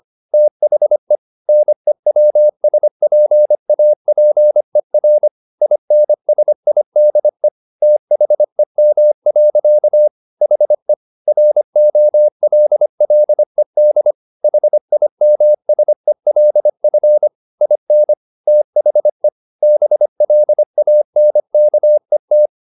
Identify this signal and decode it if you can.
Morse code, heard as a keyed tone, about 25 wpm, 600 Hz. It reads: THE NEWSPAPER INSIDE THEM. HE ROLLED HIMSELF IN THE BLANKET
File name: the old man and the sea part 1 25wpm